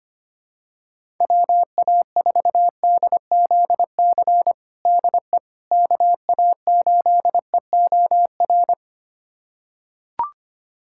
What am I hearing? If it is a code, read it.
Morse code, heard as a keyed tone, about 25 wpm, 700 Hz. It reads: WA4DZC DE KA8EOR
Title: USA Callsign DE Callsign - 6 Characters - 25wpm